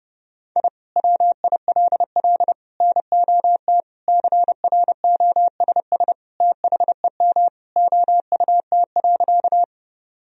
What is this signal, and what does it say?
Morse code, heard as a keyed tone, about 30 wpm, 700 Hz. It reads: I WILL NOT CROSS THEM OUT.
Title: Sentences from Top 1000 Words - Review Part 1 - 30wpm